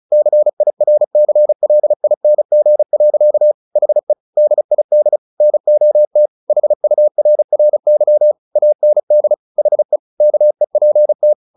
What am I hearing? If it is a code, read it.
Morse code, heard as a keyed tone, about 34 wpm, 600 Hz. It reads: CIRCLING. HE DID NOT HURRY AND HE KEPT
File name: the old man and the sea / the old man and the sea 1 35wpm